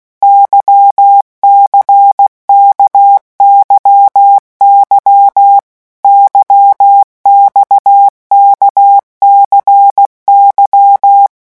Morse code, heard as a keyed tone, about 16 wpm, 800 Hz. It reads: YCKYY YXKCY
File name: morse mp3 / revision1-vitesse 15 mots